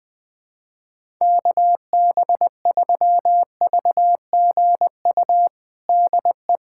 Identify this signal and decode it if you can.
Morse code, heard as a keyed tone, about 20 wpm, 700 Hz. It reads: KB3VGU DE
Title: USA Callsign DE Callsign 20wpm